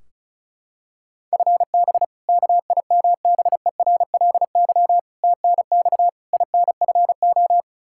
Morse code, heard as a keyed tone, about 36 wpm, 700 Hz. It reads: FB KIMBERLY TNX INFO
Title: Practice QSO - Exchange 3 - More Info - 35wpm